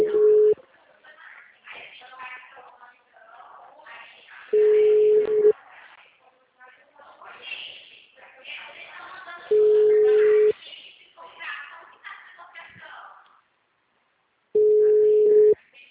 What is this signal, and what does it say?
Ringback tone (Europe, 425 Hz), with speech.
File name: out- -2021- - - .2645